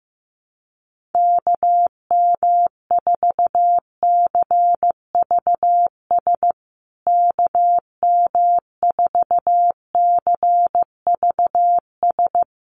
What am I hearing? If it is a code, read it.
Morse code, heard as a keyed tone, about 15 wpm, 700 Hz. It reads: KM4CVS KM4CVS
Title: USA Call Signs Repeated 3x 15wpm